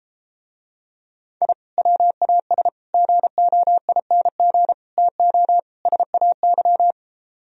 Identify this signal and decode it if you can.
Morse code, heard as a keyed tone, about 33 wpm, 700 Hz. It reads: I WAS GOING TO SAY